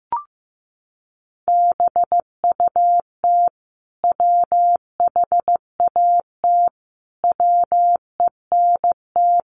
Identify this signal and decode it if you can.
Morse code, heard as a keyed tone, about 15 wpm, 700 Hz. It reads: BUT WHAT WENT
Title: Sentences from Top 1400 Words 15wpm